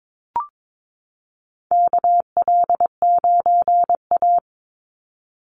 Morse code, heard as a keyed tone, about 22 wpm, 700 Hz. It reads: KL9A
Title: Call Signs Repeated 2x - 22wpm